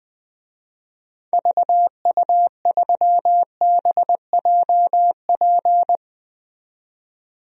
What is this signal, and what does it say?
Morse code, heard as a keyed tone, about 20 wpm, 700 Hz. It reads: VU3BJP
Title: DXCC-100 Call Signs - 6 Characters - 20wpm